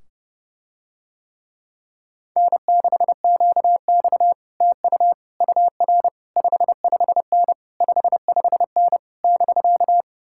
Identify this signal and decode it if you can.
Morse code, heard as a keyed tone, about 30 wpm, 700 Hz. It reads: N6QX TU UR 55N 55N <BK>